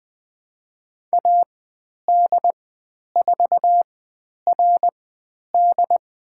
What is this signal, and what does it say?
Morse code, heard as a keyed tone, about 20 wpm, spaced as for 10 wpm, 700 Hz. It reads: AD4RD
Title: 095 Lesson 16g - Letter c - Call Signs 20wpm